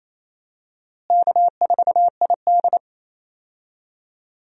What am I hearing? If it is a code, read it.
Morse code, heard as a keyed tone, about 28 wpm, 700 Hz. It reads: K4ID
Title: USA Call Signs - 4 Characters - Repeated 2x - 28wpm